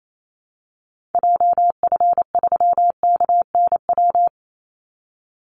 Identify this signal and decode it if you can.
Morse code, heard as a keyed tone, about 28 wpm, 700 Hz. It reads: JF3KNW
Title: Call Signs Repeated 3x - 28wpm